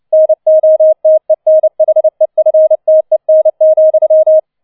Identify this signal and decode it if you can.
Morse code, heard as a keyed tone, about 29 wpm, 630 Hz. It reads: NOTENHEFTEN,